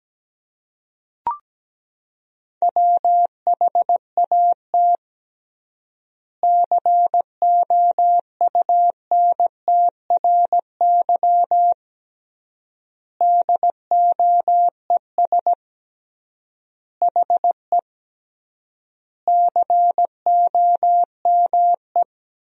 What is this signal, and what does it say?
Morse code, heard as a keyed tone, about 17 wpm, 700 Hz. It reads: WHAT COUNTRY DOES HE COME
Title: 3x Word-Spacing - Sentences from Top 300 Words - 17wpm